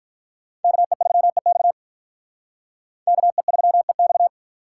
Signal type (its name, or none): Morse code